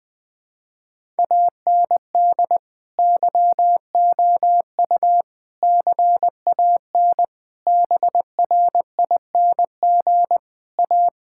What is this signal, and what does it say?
Morse code, heard as a keyed tone, about 21 wpm, 700 Hz. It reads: AND YOU CAN BRING A
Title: Sentences with 2-Word Phrases - 20wpm